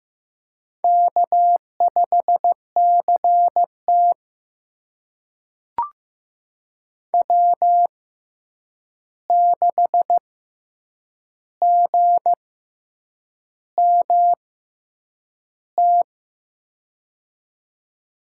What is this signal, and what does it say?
Morse code, heard as a keyed tone, about 15 wpm, spaced as for 7 wpm, 700 Hz. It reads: K5CT W6GMT
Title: USA Call Signs - 6x Character Spacing - 15wpm